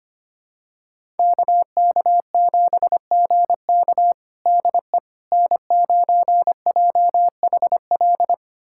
Morse code, heard as a keyed tone, about 25 wpm, 700 Hz. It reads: KK7GK DE N9JHL
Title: USA Callsign DE Callsign - 5 Characters - 25wpm